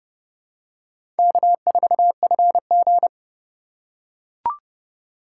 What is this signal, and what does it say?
Morse code, heard as a keyed tone, about 30 wpm, 700 Hz. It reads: K4FG
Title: USA Call Signs - 4 Characters - 3x Character Spacing - 30wpm